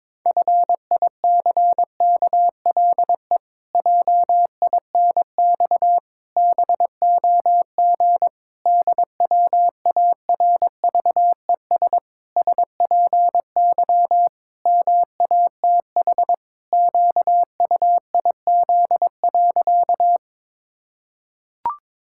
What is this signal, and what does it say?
Morse code, heard as a keyed tone, about 22 wpm, 700 Hz. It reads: FICKLE JINX BOG DWARVES SPY MATH QUIZ.
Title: Pangram Challenge v2 - 22wpm